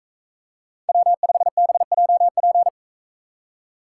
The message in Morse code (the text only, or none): W5BJP